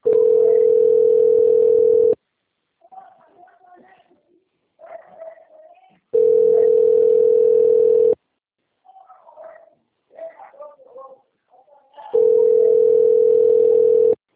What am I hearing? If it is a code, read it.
Ringback tone (North America), 440 + 480 Hz.
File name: out- - - - - .22903